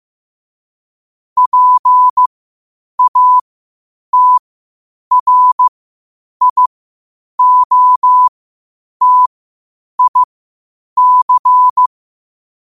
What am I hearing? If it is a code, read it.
Morse code, heard as a keyed tone, about 15 wpm, spaced as for 8 wpm, 1000 Hz. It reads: PATRIOTIC
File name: qotdfiles / 8wpm